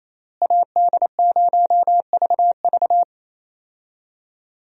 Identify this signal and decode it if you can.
Morse code, heard as a keyed tone, about 28 wpm, 700 Hz. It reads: AD0VV
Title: USA Call Signs - 5 Characters - Repeated 2x - 28wpm